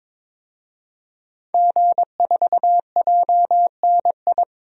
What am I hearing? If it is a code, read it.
Morse code, heard as a keyed tone, about 22 wpm, 700 Hz. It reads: G4JNI
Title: DXCC-100 Call Signs - 5 Characters - 22wpm